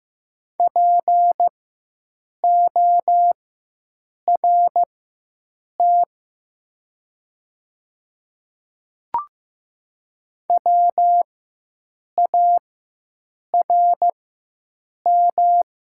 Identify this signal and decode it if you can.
Morse code, heard as a keyed tone, about 15 wpm, spaced as for 7 wpm, 700 Hz. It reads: PORT WARM
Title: 4x Character Spacing - Top 500 Words - 15wpm